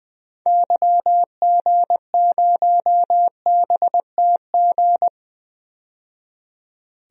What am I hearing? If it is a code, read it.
Morse code, heard as a keyed tone, about 20 wpm, 700 Hz. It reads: YG0BTG
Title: DXCC-100 Call Signs - 6 Characters - 20wpm